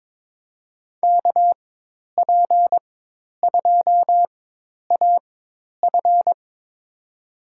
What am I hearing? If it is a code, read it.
Morse code, heard as a keyed tone, about 22 wpm, spaced as for 10 wpm, 700 Hz. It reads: KP2AF
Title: USA Call Signs - 5 Characters - 4x Character Spacing - 22wpm